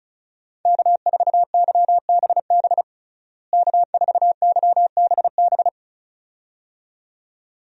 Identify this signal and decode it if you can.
Morse code, heard as a keyed tone, about 36 wpm, 700 Hz. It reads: K4YBB K4YBB
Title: USA Call Signs - 5 Characters - Repeated 2x - 35wpm